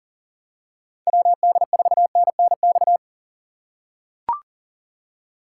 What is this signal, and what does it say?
Morse code, heard as a keyed tone, about 40 wpm, 700 Hz. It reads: WD4NNX
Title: USA Call Signs - 6 Characters - 4x Character Spacing - 40wpm